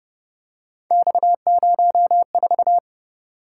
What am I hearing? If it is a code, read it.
Morse code, heard as a keyed tone, about 30 wpm, 700 Hz. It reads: X04
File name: Speed-Racing - Three Letter-Number Combinations - 25wpm